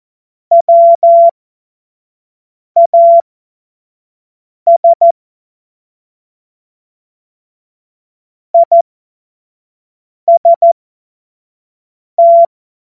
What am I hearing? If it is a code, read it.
Morse code, heard as a keyed tone, about 14 wpm, spaced as for 5 wpm, 680 Hz. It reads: WAS IST
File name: Morsekurs / Witze / Witz 28. September 2021, 25 F70